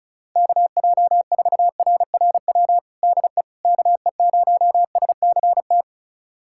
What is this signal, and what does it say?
Morse code, heard as a keyed tone, about 35 wpm, 700 Hz. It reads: KJ4RRW DE KE0SCT
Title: USA Callsign DE Callsign - 6 Characters - 35wpm